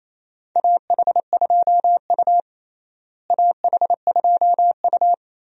Morse code, heard as a keyed tone, about 28 wpm, 700 Hz. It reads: AH2U AH2U
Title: USA Call Signs - 4 Characters - Repeated 3x - 28wpm